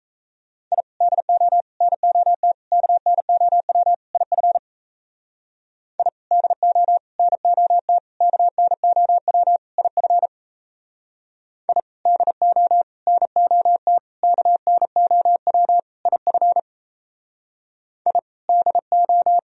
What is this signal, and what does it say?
Morse code, heard as a keyed tone, about 35 wpm, 700 Hz. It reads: I DO NOT KNOW IF I DO NOT KNOW IF I DO NOT KNOW IF I DO
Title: Speed-Racing - Sets of 5 Words 28wpm